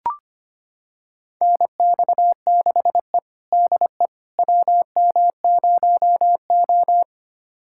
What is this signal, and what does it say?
Morse code, heard as a keyed tone, about 25 wpm, 700 Hz. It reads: NX6E DE WM0O
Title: USA Callsign DE Callsign - 4 Characters - 25wpm